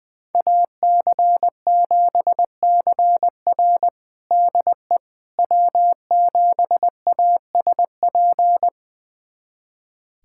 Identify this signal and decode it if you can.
Morse code, heard as a keyed tone, about 20 wpm, 700 Hz. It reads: AC7CR DE W7ASP
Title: USA Callsign DE Callsign - 5 Characters - 20wpm